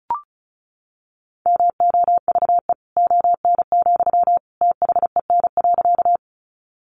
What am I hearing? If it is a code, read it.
Morse code, heard as a keyed tone, about 34 wpm, 700 Hz. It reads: MOVE ON, THEN.
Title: Sentences from Top 200 Words 35wpm